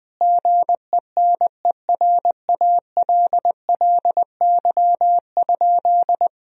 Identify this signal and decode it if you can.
Morse code, heard as a keyed tone, about 20 wpm, 700 Hz. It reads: GENERALLY?
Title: Sentences from Top 1000 Words - Review Part 4 - 20wpm